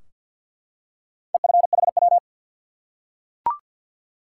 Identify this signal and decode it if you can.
Morse code, heard as a keyed tone, about 53 wpm, 700 Hz. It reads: EFHW
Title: Top 180 QSO Elements 50wpm